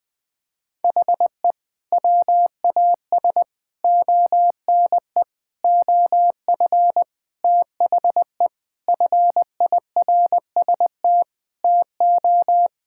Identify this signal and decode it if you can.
Morse code, heard as a keyed tone, about 20 wpm, 700 Hz. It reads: HE WAS ONE OF THE FIRST TO
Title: Sentences with 3-Word Phrases - 20wpm